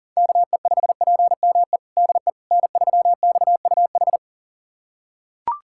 Morse code, heard as a keyed tone, about 40 wpm, 700 Hz. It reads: KE5PME DE N3XUH
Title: USA Callsign DE Callsign 40wpm